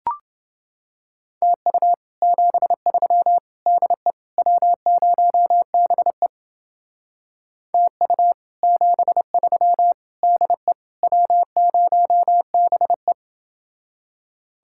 Morse code, heard as a keyed tone, about 29 wpm, 700 Hz. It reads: TU 73 DE W0BE TU 73 DE W0BE